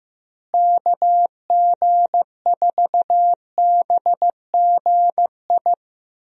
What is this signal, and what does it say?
Morse code, heard as a keyed tone, about 15 wpm, 700 Hz. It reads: KG4BGI
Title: USA Call Signs - 6 Characters - Repeated 2x - 15wpm